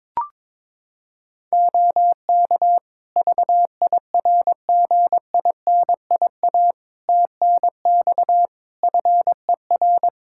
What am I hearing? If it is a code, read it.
Morse code, heard as a keyed tone, about 22 wpm, 700 Hz. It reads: OK VIRGINIA TNX FER
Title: Practice QSO - Exchange 4 - Ending - 22wpm